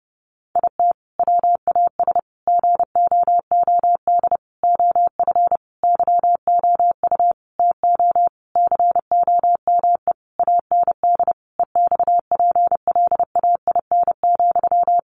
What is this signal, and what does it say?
Morse code, heard as a keyed tone, about 30 wpm, 700 Hz. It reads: IT WAS GOOD OF YOU TO COME AND EXPLAIN,
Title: Sentences from Top 1100 Words 30wpm